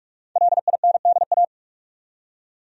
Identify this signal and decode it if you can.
Morse code, heard as a keyed tone, about 47 wpm, 700 Hz. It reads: LINDA
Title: USA Common Names - Max 5 Letters - 45wpm